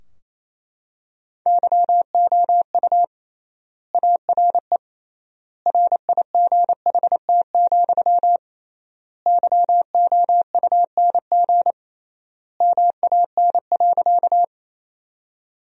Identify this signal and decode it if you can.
Morse code, heard as a keyed tone, about 28 wpm, 700 Hz. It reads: YOU ARE RIGHT, YOUNG MAN.
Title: 3x Word-Spacing - Sentences from Top 400 Words - 28wpm